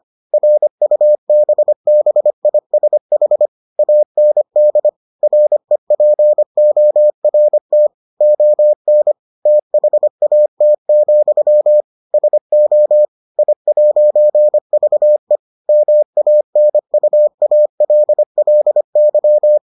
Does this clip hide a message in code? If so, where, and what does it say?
Yes: Morse code from 0.3 to 19.6 s, heard as a keyed tone, about 25 wpm, 600 Hz. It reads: RUBBISH AND REPORT ON THAT, SO I'VE MANUALLY